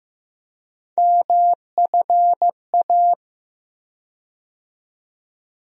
Morse code, heard as a keyed tone, about 15 wpm, 700 Hz. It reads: MFA
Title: USA Call Sign Postfixes - 3 Letters 15wpm